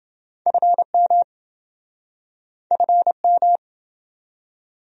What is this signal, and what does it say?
Morse code, heard as a keyed tone, about 28 wpm, 700 Hz. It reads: FM FM